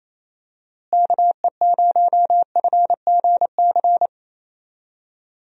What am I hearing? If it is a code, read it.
Morse code, heard as a keyed tone, about 28 wpm, 700 Hz. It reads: KE0FGC
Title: USA Call Signs - 6 Characters - Repeated 3x - 28wpm